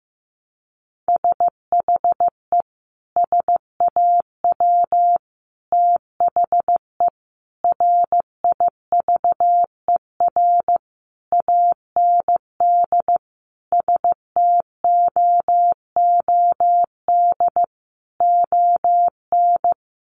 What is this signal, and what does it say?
Morse code, heard as a keyed tone, about 15 wpm, 700 Hz. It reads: SHE SAW THE RIVER AND STOOD ON
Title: Sentences from Top 1200 Words 15wpm